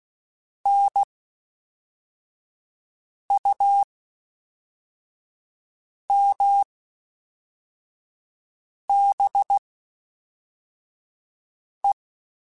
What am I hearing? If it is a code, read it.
Morse code, heard as a keyed tone, about 16 wpm, spaced as for 4 wpm, 800 Hz. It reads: NUMBE
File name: Special INGs / Special INGs-mp3s / Fast ing @10